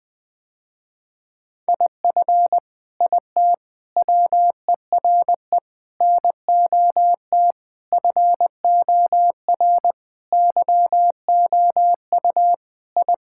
Morse code, heard as a keyed tone, about 21 wpm, 700 Hz. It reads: IF IT WERE NOT FOR YOU I
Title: Sentences with 5-Word Phrases - Spoken Beginning - 20wpm